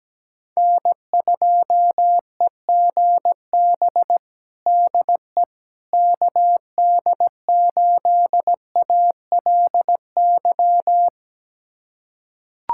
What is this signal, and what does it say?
Morse code, heard as a keyed tone, about 17 wpm, 700 Hz. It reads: N2EGB DE KD8ALY
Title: USA Callsign DE Callsign 17wpm